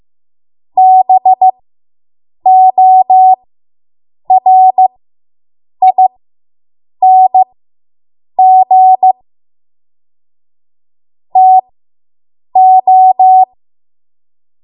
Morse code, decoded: BORING TO